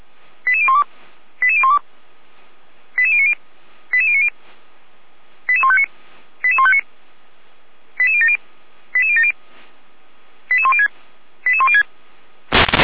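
Five-tone selective call (ZVEI), heard as quick runs of five tones, tones of about 70 ms. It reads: 88012 88012 88090 88090 88159 88159 88070 88070 88106 88106
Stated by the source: ZVEI five-tone call 88199 received on SDR by feuerwehr-orlamuende fire brigade.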